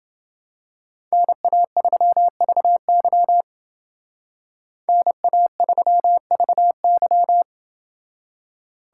Morse code, heard as a keyed tone, about 28 wpm, 700 Hz. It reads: NA3VY NA3VY